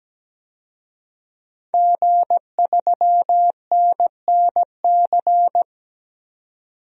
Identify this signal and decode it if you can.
Morse code, heard as a keyed tone, about 17 wpm, 700 Hz. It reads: G3NNC
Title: DXCC-100 Call Signs - 5 Characters - 17wpm